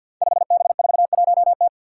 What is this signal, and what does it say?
Morse code, heard as a keyed tone, about 52 wpm, 700 Hz. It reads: 5B41T
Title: Five Challenging Letter-Number Combinations 50wpm